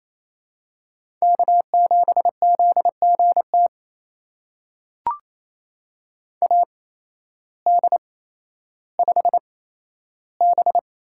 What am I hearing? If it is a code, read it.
Morse code, heard as a keyed tone, about 28 wpm, spaced as for 13 wpm, 700 Hz. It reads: K7ZGT AD5B
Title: USA Call Signs - 8x Character Spacing - 28wpm